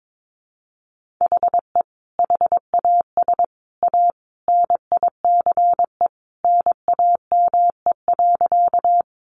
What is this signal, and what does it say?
Morse code, heard as a keyed tone, about 22 wpm, 700 Hz. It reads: HE HAS A NICE NAME.
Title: Sentences from Top 1100 Words 22wpm